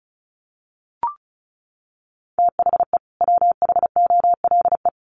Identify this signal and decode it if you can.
Morse code, heard as a keyed tone, about 34 wpm, 700 Hz. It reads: THE WHOLE
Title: Sets of 2 Words 35wpm